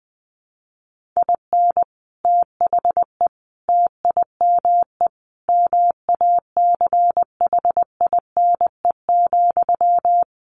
Morse code, heard as a keyed tone, about 20 wpm, 700 Hz. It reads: IN THE TIME MACHINE,
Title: A Sound Of Thunder 20wpm